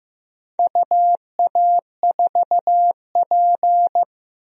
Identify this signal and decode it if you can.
Morse code, heard as a keyed tone, about 15 wpm, 700 Hz. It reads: UA4P
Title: DXCC-100 Call Signs - 4 Characters - Repeated 2x - 15wpm